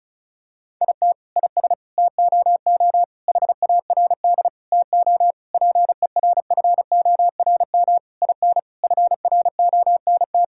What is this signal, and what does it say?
Morse code, heard as a keyed tone, about 36 wpm, 700 Hz. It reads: IT IS TOO HARD TO PERFORM IN FRONT
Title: Sentences from Top 1000 Words - Review Part 5 - 35wpm